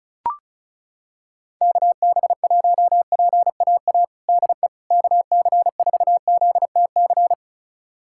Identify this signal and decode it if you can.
Morse code, heard as a keyed tone, about 36 wpm, 700 Hz. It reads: KB1PAA DE KC4ZTC
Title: USA Callsign DE Callsign - 6 Characters - 35wpm